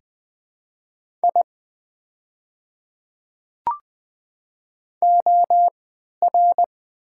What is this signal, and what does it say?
Morse code, heard as a keyed tone, about 20 wpm, spaced as for 11 wpm, 700 Hz. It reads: I OR
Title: 3x Character Spacing - Top 100 Words - 20wpm